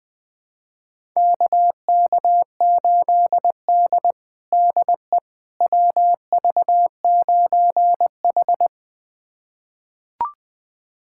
Morse code, heard as a keyed tone, about 20 wpm, 700 Hz. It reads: KK8D DE WV9H
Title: USA Callsign DE Callsign - 4 Characters - 20wpm